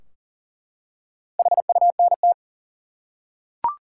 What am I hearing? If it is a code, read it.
Morse code, heard as a keyed tone, about 40 wpm, 700 Hz. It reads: HUNT
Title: Four Letter Words - 40wpm